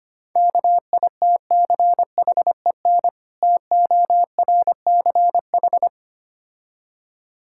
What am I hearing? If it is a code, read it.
Morse code, heard as a keyed tone, about 25 wpm, 690 Hz. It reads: KITCHEN TORCH
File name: practice / 25wpm / Difficult Words - Speed-Racing with Hints - 25wpm